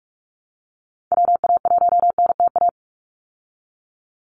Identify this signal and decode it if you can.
Morse code, heard as a keyed tone, about 45 wpm, 700 Hz. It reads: WA1NTA
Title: Call Signs Repeated 3x - 45wpm